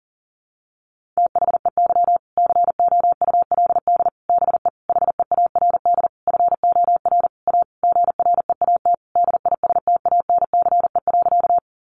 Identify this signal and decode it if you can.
Morse code, heard as a keyed tone, about 40 wpm, 700 Hz. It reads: THEY COULD BE HEARD FOR A GREAT DISTANCE.
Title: Sentences from Top 1200 Words 40wpm